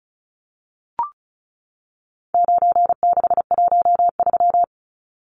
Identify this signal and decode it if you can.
Morse code, heard as a keyed tone, about 34 wpm, 700 Hz. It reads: 9613
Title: Numbers - 4-Digit 35wpm